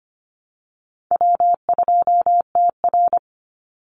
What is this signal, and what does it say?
Morse code, heard as a keyed tone, about 25 wpm, 700 Hz. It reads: W2TR
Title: Easy Call Signs Repeated 2x - 25wpm